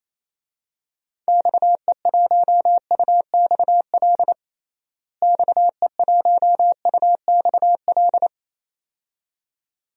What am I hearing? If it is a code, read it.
Morse code, heard as a keyed tone, about 28 wpm, 700 Hz. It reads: XE1UXL XE1UXL
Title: DXCC-100 Call Signs - 6 Characters - Repeated 2x - 28wpm